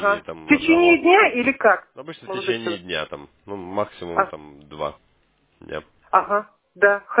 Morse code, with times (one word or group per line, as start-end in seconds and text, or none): none